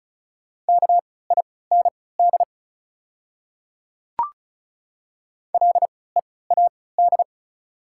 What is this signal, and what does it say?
Morse code, heard as a keyed tone, about 36 wpm, spaced as for 20 wpm, 690 Hz. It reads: KIND LEAD
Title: 3x Character Spacing - Top 500 Words - 35wpm